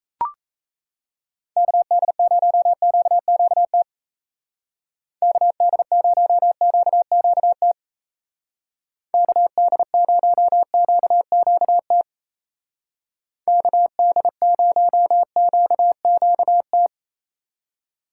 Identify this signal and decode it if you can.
Morse code, heard as a keyed tone, about 35 wpm, 700 Hz. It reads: KD0QQT KD0QQT KD0QQT KD0QQT
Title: Speed Racing - USA 2x3 Call Signs 28wpm